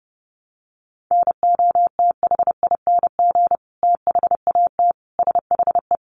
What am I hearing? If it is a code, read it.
Morse code, heard as a keyed tone, about 30 wpm, 700 Hz. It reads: NOTHING THAT SHE
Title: Sentences from Top 1100 Words 30wpm